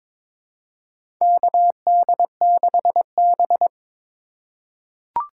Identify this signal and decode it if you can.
Morse code, heard as a keyed tone, about 21 wpm, 700 Hz. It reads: KD6B
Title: USA Call Signs - 3x Character Spacing - 22wpm